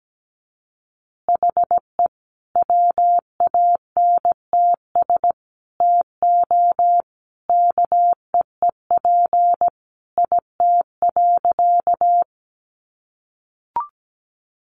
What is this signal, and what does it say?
Morse code, heard as a keyed tone, about 17 wpm, 700 Hz. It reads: HE WANTS TO KEEP IT.
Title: Additional Easy Phrases 17wpm